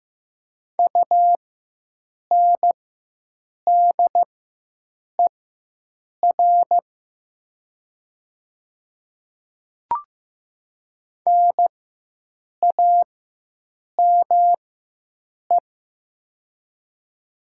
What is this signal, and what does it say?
Morse code, heard as a keyed tone, about 15 wpm, spaced as for 7 wpm, 700 Hz. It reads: UNDER NAME
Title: 4x Character Spacing - Top 200 Words - 15wpm